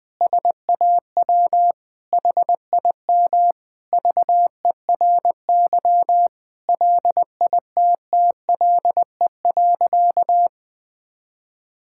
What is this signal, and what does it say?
Morse code, heard as a keyed tone, about 20 wpm, 700 Hz. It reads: SAW HIM VERY LITTLE.
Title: Sentences from Top 1000 Words - Review Part 5 - 20wpm